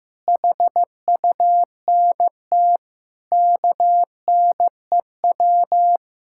Morse code, heard as a keyed tone, about 15 wpm, 700 Hz. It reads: HUNT KNEW
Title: Sentences from Top 1600 Words 15wpm